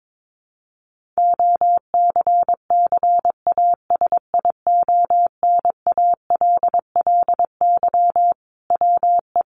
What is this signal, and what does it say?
Morse code, heard as a keyed tone, about 22 wpm, 700 Hz. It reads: OCCASIONALLY WE
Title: Sentences from Top 1300 Words 22wpm